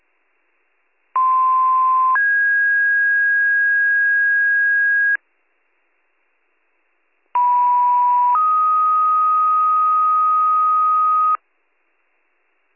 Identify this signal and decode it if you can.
Two-tone page, heard as a one-second tone, then a three-second tone, about 1040 then 1650 Hz, about 980 then 1250 Hz. It is heard from 1.1 to 11.4 s.